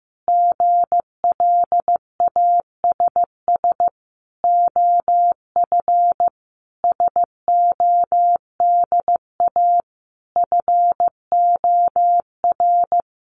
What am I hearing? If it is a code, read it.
Morse code, heard as a keyed tone, about 15 wpm, 700 Hz. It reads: GLASS OF SODA FOR
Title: Additional Easy Phrases 15wpm